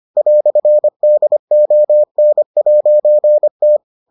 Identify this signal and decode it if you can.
Morse code, heard as a keyed tone, about 25 wpm, 600 Hz. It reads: "DON'T